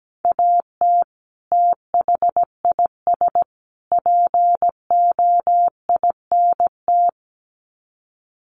Morse code, heard as a keyed tone, about 17 wpm, 700 Hz. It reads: AT THIS POINT
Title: Sets of 3 Words 17wpm